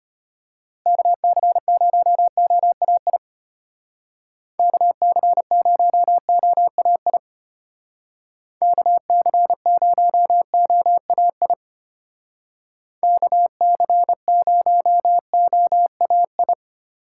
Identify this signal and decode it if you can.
Morse code, heard as a keyed tone, about 30 wpm, 700 Hz. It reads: KC0OAI KC0OAI KC0OAI KC0OAI
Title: Speed Racing - USA 2x3 Call Signs 25wpm